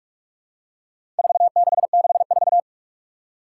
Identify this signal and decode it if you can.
Morse code, heard as a keyed tone, about 46 wpm, 700 Hz. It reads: 4664
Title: Four Challenging Letter-Number Combinations 45wpm